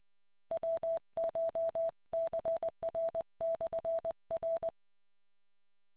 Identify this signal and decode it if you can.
Morse code, heard as a keyed tone, about 20 wpm, 660 Hz. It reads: WCHCR/R